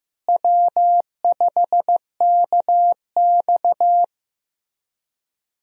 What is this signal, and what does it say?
Morse code, heard as a keyed tone, about 15 wpm, 700 Hz. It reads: W5KX
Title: USA Call Signs - 4 Characters - Repeated 3x - 15wpm